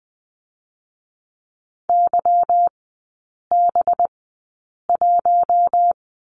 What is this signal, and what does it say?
Morse code, heard as a keyed tone, about 20 wpm, spaced as for 8 wpm, 700 Hz. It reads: YB1
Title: 205 Lesson 33e - Letter j - Three Characters 20wpm